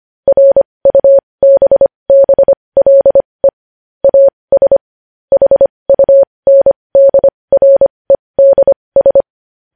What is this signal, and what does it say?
Morse code, heard as a keyed tone, about 25 wpm, spaced as for 20 wpm, 550 Hz. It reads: RUBBLE AS HUNDREDS
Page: news-headlines-in-morse-code-at-20-wpm